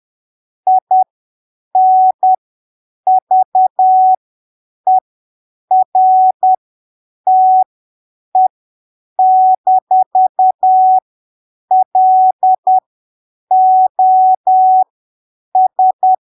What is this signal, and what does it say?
Morse code, heard as a keyed tone, about 10 wpm, spaced as for 7 wpm, 750 Hz. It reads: INVERTE-LOS
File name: arquivos / texto / texto 15 10ppm